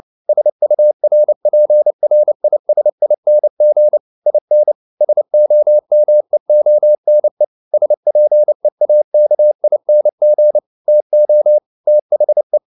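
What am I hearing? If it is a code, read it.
Morse code, heard as a keyed tone, about 29 wpm, 600 Hz. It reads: SURPRISING IN SOMEONE SPEAKING TO THE
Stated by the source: Morse code practice at 28 wpm, text of Ken Follett's novel Never in English, from CW med Gnister.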